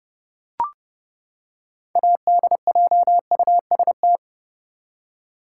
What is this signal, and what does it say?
Morse code, heard as a keyed tone, about 31 wpm, 700 Hz. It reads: ADJUST
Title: Six Letter Words - 30wpm